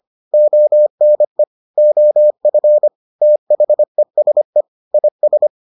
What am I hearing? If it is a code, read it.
Morse code, heard as a keyed tone, about 25 wpm, 600 Hz. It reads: ONE OF THESE IS